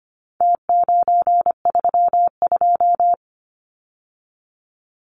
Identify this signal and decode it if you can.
Morse code, heard as a keyed tone, about 25 wpm, 700 Hz. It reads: T932
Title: Numbers - Contest Style - 4-Digit 25wpm